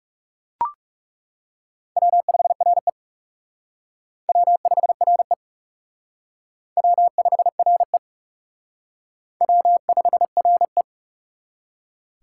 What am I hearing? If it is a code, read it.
Morse code, heard as a keyed tone, about 39 wpm, 700 Hz. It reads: W5RE W5RE W5RE W5RE